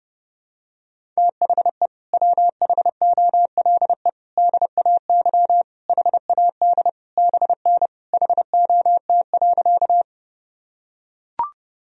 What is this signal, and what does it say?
Morse code, heard as a keyed tone, about 32 wpm, 700 Hz. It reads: THE WHOLE DAY HAD BN HOT.
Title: Sentences from Top 500 Words and Abbreviations 30wpm